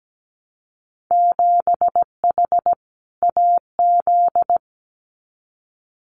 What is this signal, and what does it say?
Morse code, heard as a keyed tone, about 17 wpm, 700 Hz. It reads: 7H AZ
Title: Winter Field Day Exchanges Encore 17wpm